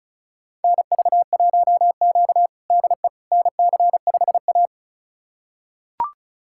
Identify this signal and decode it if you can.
Morse code, heard as a keyed tone, about 36 wpm, 690 Hz. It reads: NV1Q DE NC5A
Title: USA Callsign DE Callsign - 4 Characters - 35wpm